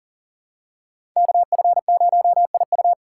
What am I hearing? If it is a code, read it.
Morse code, heard as a keyed tone, about 40 wpm, 700 Hz. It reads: KF0IU
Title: USA Call Signs - 5 Characters - 3x Character Spacing - 40wpm